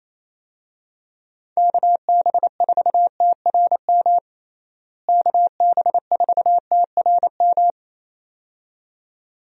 Morse code, heard as a keyed tone, about 27 wpm, 700 Hz. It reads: KB4TRM KB4TRM
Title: USA Call Signs - 6 Characters - Repeated 2x - 28wpm